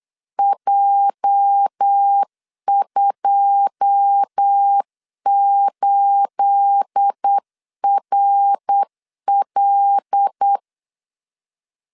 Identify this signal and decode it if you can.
Morse code, heard as a keyed tone, about 8 wpm, 800 Hz. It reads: J28RL
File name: LEZIONI / 9 28 8wpm 3512